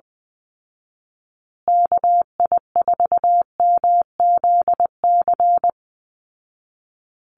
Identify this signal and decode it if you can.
Morse code, heard as a keyed tone, about 19 wpm, 700 Hz. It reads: KI4MZC
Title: Call Signs Repeated 2x - 20wpm